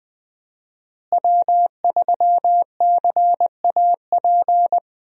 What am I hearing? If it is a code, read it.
Morse code, heard as a keyed tone, about 21 wpm, 700 Hz. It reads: W3CAP
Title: USA Call Signs - 5 Characters - Repeated 3x - 20wpm